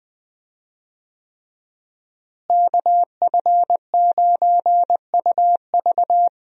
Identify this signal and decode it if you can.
Morse code, heard as a keyed tone, about 20 wpm, 700 Hz. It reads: KF9UV
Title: USA Call Signs - 5 Characters - Repeated 3x - 20wpm